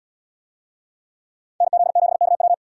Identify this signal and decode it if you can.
Morse code, heard as a keyed tone, about 77 wpm, 700 Hz. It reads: NZ7KW